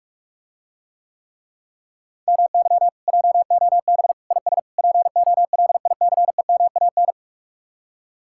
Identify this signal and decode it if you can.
Morse code, heard as a keyed tone, about 50 wpm, 700 Hz. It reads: MY JOB IS POLICEMAN